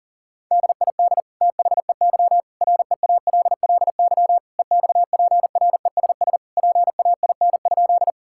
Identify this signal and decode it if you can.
Morse code, heard as a keyed tone, about 37 wpm, 700 Hz. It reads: DID THEY REALLY EXPRESS PAIN?
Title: Sentences from Top 1600 Words 40wpm